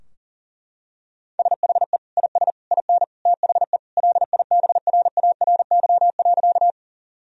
Morse code, heard as a keyed tone, about 37 wpm, 700 Hz. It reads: SHE IS IN THE LIBRARY.
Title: Sentences from Top 1000 Words - Review Part 4 - 40wpm